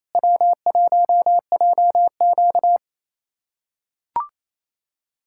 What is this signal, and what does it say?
Morse code, heard as a keyed tone, about 28 wpm, 700 Hz. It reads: W1JQ
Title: USA Callsign DE Callsign - 4 Characters - 28wpm